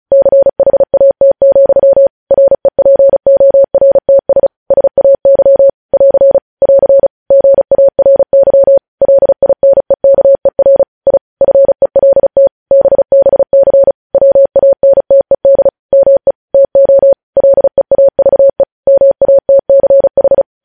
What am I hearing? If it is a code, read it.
Morse code, heard as a keyed tone, about 36 wpm, 550 Hz. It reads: CHAT, REPORTS SAY <AR> <AR> GARY LINEKER I FELT BBC WANTED ME TO LEAVE MATCH
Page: news headlines in morse code at 35 wpm